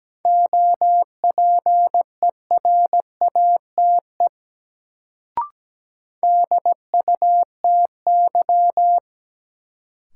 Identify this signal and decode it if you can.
Morse code, heard as a keyed tone, about 17 wpm, 700 Hz. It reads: OPERATE DUTY
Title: Sentences from Top 1500 Words - New Word Drill Alt - 17wpm